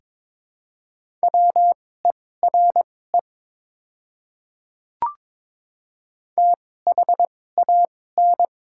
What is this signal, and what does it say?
Morse code, heard as a keyed tone, about 22 wpm, spaced as for 16 wpm, 700 Hz. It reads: WERE THAN
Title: 2x Character Spacing - Top 300 Words - 22wpm